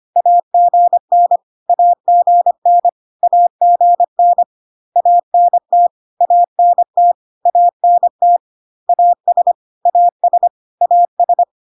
Morse code, heard as a keyed tone, about 25 wpm, 700 Hz. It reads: AGN AGN AGN ANT ANT ANT AS AS AS
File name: cw / CW common abbreviations alphabetical repeated 3 times 25 25wpm ik1hge